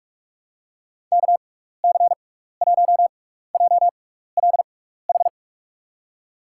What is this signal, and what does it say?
Morse code, heard as a keyed tone, about 44 wpm, spaced as for 15 wpm, 700 Hz. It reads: KC1JLH
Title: USA Call Signs - 6 Characters - 6x Character Spacing - 45wpm